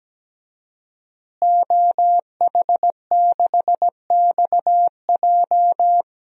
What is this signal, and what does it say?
Morse code, heard as a keyed tone, about 17 wpm, 700 Hz. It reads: OH6XJ
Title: DXCC-100 Call Signs - 5 Characters - Repeated 2x - 17wpm